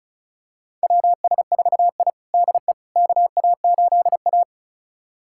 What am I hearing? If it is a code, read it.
Morse code, heard as a keyed tone, about 34 wpm, 700 Hz. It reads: WS4I DE KA8A
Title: USA Callsign DE Callsign - 4 Characters - 35wpm